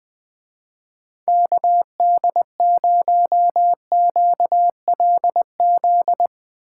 Morse code, heard as a keyed tone, about 21 wpm, 700 Hz. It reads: KD0QLZ